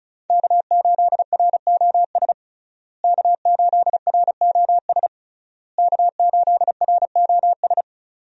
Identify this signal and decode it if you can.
Morse code, heard as a keyed tone, about 34 wpm, 700 Hz. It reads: K8ROS K8ROS K8ROS
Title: USA Call Signs - 5 Characters - Repeated 3x - 35wpm